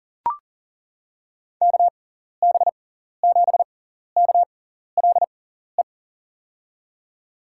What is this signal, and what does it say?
Morse code, heard as a keyed tone, about 40 wpm, spaced as for 14 wpm, 700 Hz. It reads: KB7KLE
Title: USA Call Signs - 6 Characters - 6x Character Spacing - 40wpm